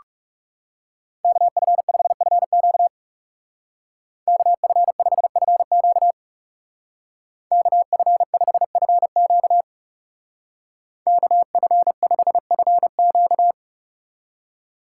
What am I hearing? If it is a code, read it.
Morse code, heard as a keyed tone, about 36 wpm, 700 Hz. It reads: KF5FQ KF5FQ KF5FQ KF5FQ
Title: Speed Racing - USA 2x2 Call Signs 30wpm